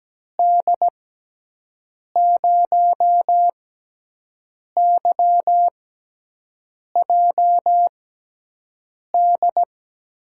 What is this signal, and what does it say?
Morse code, heard as a keyed tone, about 17 wpm, spaced as for 6 wpm, 700 Hz. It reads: D0YJD
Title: USA Call Signs - 6x Character Spacing - 17wpm